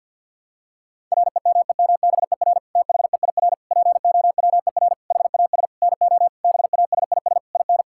Morse code, heard as a keyed tone, about 51 wpm, 690 Hz. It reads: REMEMBER THEIR POWER HAS NO BASIS IN
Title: Sentences from Top 1000 Words - Review Part 5 - 50wpm